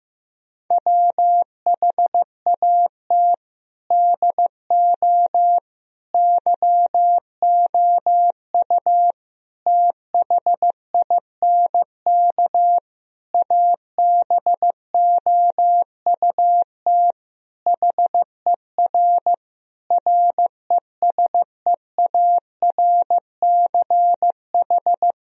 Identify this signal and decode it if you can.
Morse code, heard as a keyed tone, about 15 wpm, 700 Hz. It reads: WHAT DO YOU THINK ABOUT HER RESEARCH
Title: Sentences from Top 1000 Words - Review Part 4 - 15wpm